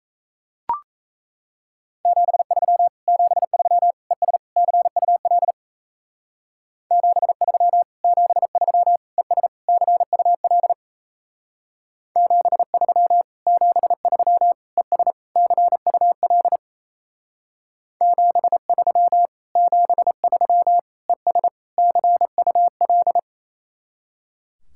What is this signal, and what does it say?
Morse code, heard as a keyed tone, about 37 wpm, 700 Hz. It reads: 73 73 ES CUL 73 73 ES CUL 73 73 ES CUL 73 73 ES CUL